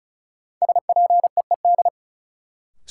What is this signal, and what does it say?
Morse code, heard as a keyed tone, about 34 wpm, 700 Hz. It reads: SPEED
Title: Sentences from Top 800 Words - New Word Drill - 35wpm